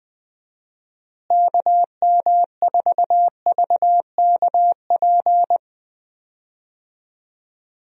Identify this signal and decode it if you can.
Morse code, heard as a keyed tone, about 20 wpm, 700 Hz. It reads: KM4VKP
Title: USA Call Signs - 6 Characters - Repeated 2x - 20wpm